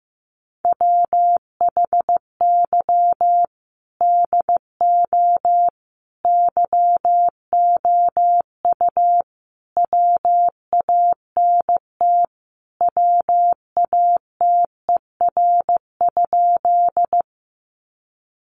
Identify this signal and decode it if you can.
Morse code, heard as a keyed tone, about 14 wpm, 700 Hz. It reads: WHY DO YOU WANT WATER?
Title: Sentences from Top 200 Words 15wpm